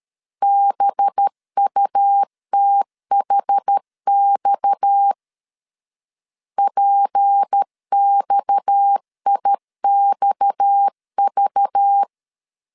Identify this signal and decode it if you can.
Morse code, heard as a keyed tone, about 13 wpm, 800 Hz. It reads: BUTHX PXIXV
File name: LEZIONI / 6 PKX0 / 6 PKX 12wpm 2024